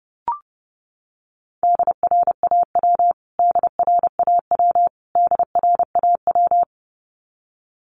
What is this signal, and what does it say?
Morse code, heard as a keyed tone, about 30 wpm, 700 Hz. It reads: DRAW DRAW DRAW
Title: Top 500 Words Repeated 30wpm